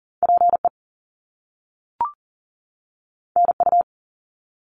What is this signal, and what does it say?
Morse code, heard as a keyed tone, about 40 wpm, 700 Hz. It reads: PE NU
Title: Canadian Provinces and Territories 40wpm